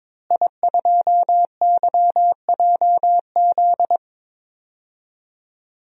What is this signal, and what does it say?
Morse code, heard as a keyed tone, about 22 wpm, 700 Hz. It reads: I2YJZ
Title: DXCC-100 Call Signs - 5 Characters - Repeated 3x - 22wpm